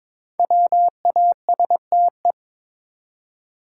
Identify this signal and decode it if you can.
Morse code, heard as a keyed tone, about 22 wpm, 690 Hz. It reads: WASTE
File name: 5 Letter Words - First 3 Letters Given - 22wpm